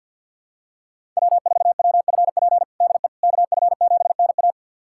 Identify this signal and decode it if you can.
Morse code, heard as a keyed tone, about 52 wpm, 700 Hz. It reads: W4WFP DE KF7NA
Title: USA Callsign DE Callsign - 5 Characters - 50wpm